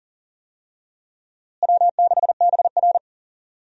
Morse code, heard as a keyed tone, about 40 wpm, 700 Hz. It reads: W6BR
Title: USA Call Signs - 4 Characters - Repeated 2x - 40wpm